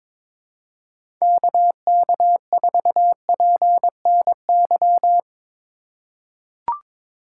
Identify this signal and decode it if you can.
Morse code, heard as a keyed tone, about 22 wpm, 700 Hz. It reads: KK4PNY
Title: USA Call Signs - 4x Character Spacing - 22wpm